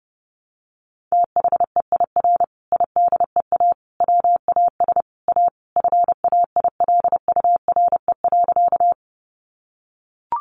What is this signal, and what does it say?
Morse code, heard as a keyed tone, about 30 wpm, 700 Hz. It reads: THEIR IDEA WAS A FAILURE.
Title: Sentences from Top 900 Words 30wpm